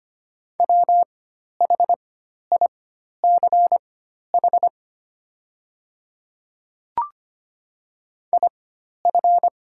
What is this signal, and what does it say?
Morse code, heard as a keyed tone, about 25 wpm, spaced as for 12 wpm, 700 Hz. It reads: WHICH IF
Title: 4x Character Spacing - Top 100 Words - 25wpm